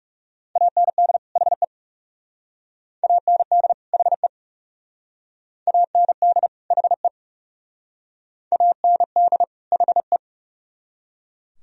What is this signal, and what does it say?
Morse code, heard as a keyed tone, about 40 wpm, 700 Hz. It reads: AND HE AND HE AND HE AND HE